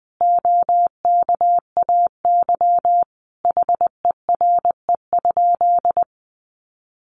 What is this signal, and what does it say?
Morse code, heard as a keyed tone, about 20 wpm, 700 Hz. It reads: OKAY HERE?
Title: A Sound Of Thunder 20wpm